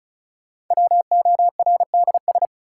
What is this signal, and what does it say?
Morse code, heard as a keyed tone, about 36 wpm, 700 Hz. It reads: WORDS
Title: Sets of 3 Words - New Word Drill - 35wpm